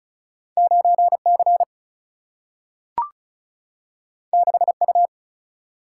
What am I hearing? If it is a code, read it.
Morse code, heard as a keyed tone, about 36 wpm, 700 Hz. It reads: 9 C 6 U
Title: Single-Letter Single-Number 35wpm